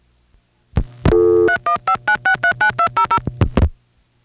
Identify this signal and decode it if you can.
DTMF, heard as telephone touch tones, tones of about 90 ms. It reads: 3159639200